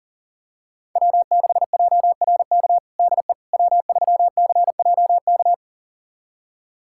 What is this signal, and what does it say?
Morse code, heard as a keyed tone, about 40 wpm, 700 Hz. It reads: W6JRK DE W3CJK
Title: USA Callsign DE Callsign - 5 Characters - 40wpm